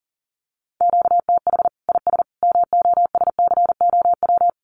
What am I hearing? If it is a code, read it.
Morse code, heard as a keyed tone, about 40 wpm, 700 Hz. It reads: QTH IS MOSCOW
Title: Common QSO Phrases 40wpm